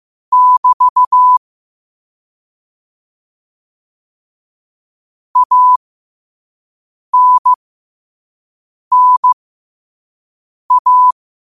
Morse code, heard as a keyed tone, about 15 wpm, spaced as for 5 wpm, 1000 Hz. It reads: <BT> ANNA
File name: qotdfiles / 5wpm